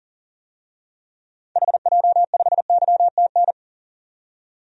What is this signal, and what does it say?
Morse code, heard as a keyed tone, about 40 wpm, 700 Hz. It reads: HJ5YTN
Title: DXCC-100 Call Signs - 6 Characters - 40wpm